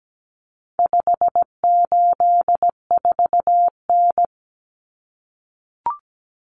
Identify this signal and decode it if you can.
Morse code, heard as a keyed tone, about 17 wpm, 700 Hz. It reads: 584N
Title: Common Cut Numbers - 4-Digit 17wpm